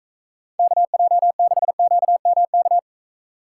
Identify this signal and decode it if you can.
Morse code, heard as a keyed tone, about 42 wpm, 700 Hz. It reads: KJ6QMK